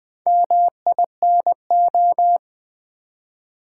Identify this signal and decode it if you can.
Morse code, heard as a keyed tone, about 20 wpm, 700 Hz. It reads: MINO
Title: Build A Word - 1 Letter At A Time - 5-Letter Words - 20wpm